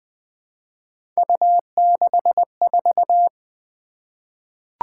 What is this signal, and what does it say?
Morse code, heard as a keyed tone, about 20 wpm, 700 Hz. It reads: U64
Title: Three Letter-Number Combinations - 20wpm